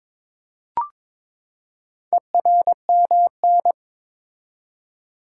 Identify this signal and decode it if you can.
Morse code, heard as a keyed tone, about 23 wpm, 700 Hz. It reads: ERMN
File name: practice / 22wpm / Four Letter-Number Combinations - 22wpm